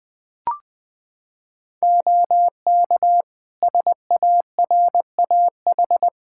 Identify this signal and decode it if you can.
Morse code, heard as a keyed tone, about 20 wpm, 700 Hz. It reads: OK SARAH
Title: Practice QSO - Exchange 4 - Ending - 20wpm